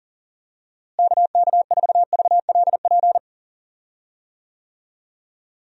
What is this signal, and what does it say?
Morse code, heard as a keyed tone, about 40 wpm, 700 Hz. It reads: KK4VLP
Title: USA Call Signs - 6 Characters - Repeated 3x - 40wpm